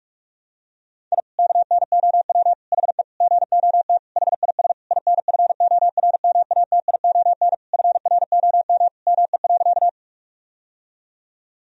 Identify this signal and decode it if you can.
Morse code, heard as a keyed tone, about 44 wpm, 700 Hz. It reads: I KNOW HE GOT HIS INFORMATION FROM ME.
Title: Sentences from Top 1000 Words - Review Part 3 - 45wpm